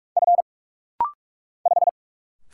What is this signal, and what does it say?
Morse code, heard as a keyed tone, about 46 wpm, 700 Hz. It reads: F 5
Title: Single Letter-Number Alt - Rapid-Fire 45wpm